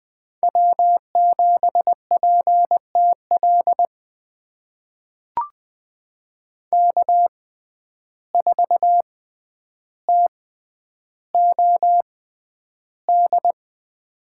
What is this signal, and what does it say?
Morse code, heard as a keyed tone, about 20 wpm, spaced as for 10 wpm, 700 Hz. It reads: W7PTL K4TOD
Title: USA Call Signs - 5 Characters - 6x Character Spacing - 20wpm